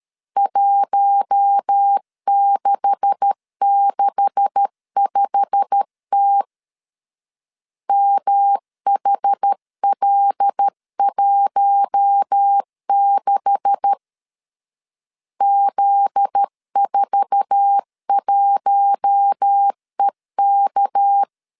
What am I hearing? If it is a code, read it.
Morse code, heard as a keyed tone, about 13 wpm, 800 Hz. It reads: 1665T MHL16 Z41EK
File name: LEZIONI / 9 286 12wpm 2305